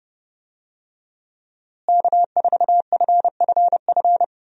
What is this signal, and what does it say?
Morse code, heard as a keyed tone, about 32 wpm, 700 Hz. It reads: K4FFF
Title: USA Call Signs - 5 Characters - Repeated 3x - 30wpm